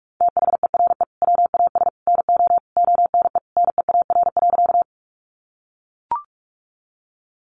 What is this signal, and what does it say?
Morse code, heard as a keyed tone, about 44 wpm, 700 Hz. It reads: THERE WAS NO ONE NEAR.
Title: Sentences from Top 200 Words 45wpm